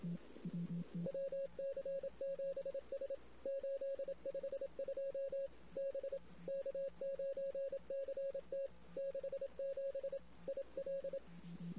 Morse code, heard as a keyed tone, about 27 wpm, 550 Hz. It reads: WC7S 852 B K9CT 67 IL
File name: CW / 2008-ss-cw-k5tr-op-n5rz -16:00